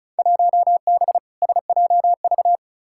Morse code, heard as a keyed tone, about 35 wpm, 700 Hz. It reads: 1B SJV
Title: Field Day Exchanges Encore - 35wpm